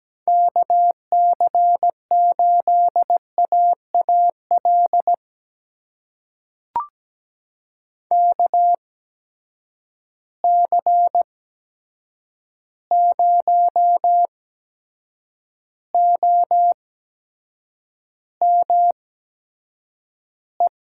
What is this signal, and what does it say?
Morse code, heard as a keyed tone, about 17 wpm, spaced as for 7 wpm, 700 Hz. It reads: KC8AAL KC0OME
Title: USA Call Signs - 6 Characters - 8x Character Spacing - 17wpm